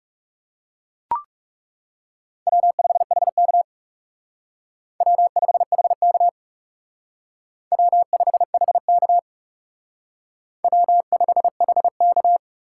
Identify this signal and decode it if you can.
Morse code, heard as a keyed tone, about 38 wpm, 700 Hz. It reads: W5HK W5HK W5HK W5HK